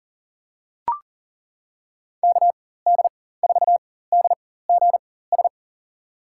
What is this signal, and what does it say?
Morse code, heard as a keyed tone, about 40 wpm, spaced as for 19 wpm, 700 Hz. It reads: KD4DGS
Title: USA Call Signs - 4x Character Spacing - 40wpm